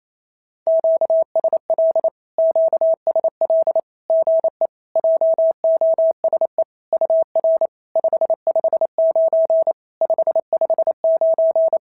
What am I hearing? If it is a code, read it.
Morse code, heard as a keyed tone, about 28 wpm, 650 Hz. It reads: QSL QSL GE JOSE UR 559 559
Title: SOTA QSO Practice 28wpm